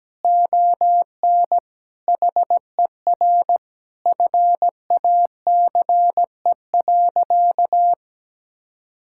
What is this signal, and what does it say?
Morse code, heard as a keyed tone, about 17 wpm, 700 Hz. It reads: ON HER FACE.
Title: Sentences from Top 1600 Words 17wpm